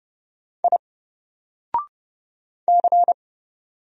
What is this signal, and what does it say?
Morse code, heard as a keyed tone, about 30 wpm, 700 Hz. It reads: I C
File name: Single-Letter-Number-30wpm